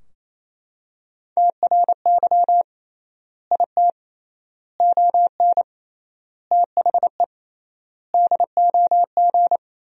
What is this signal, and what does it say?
Morse code, heard as a keyed tone, about 28 wpm, 700 Hz. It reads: TRY IT ON THE DOG
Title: 3x Word-Spacing - Sentences from Top 400 Words - 28wpm